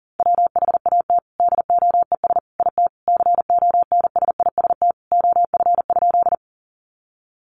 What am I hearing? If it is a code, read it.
Morse code, heard as a keyed tone, about 38 wpm, 700 Hz. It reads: WHAT DOES IT CONSIST OF?
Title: Sentences from Top 1400 Words 40wpm